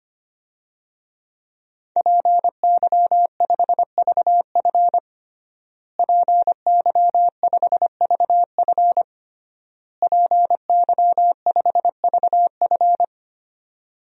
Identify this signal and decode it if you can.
Morse code, heard as a keyed tone, about 25 wpm, 700 Hz. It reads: PY5VF PY5VF PY5VF
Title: DXCC-100 Call Signs - 5 Characters - Repeated 3x - 25wpm